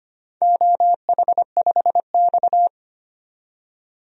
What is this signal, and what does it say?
Morse code, heard as a keyed tone, about 25 wpm, 700 Hz. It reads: OH5X
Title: DXCC-100 Call Signs - 4 Characters - 25wpm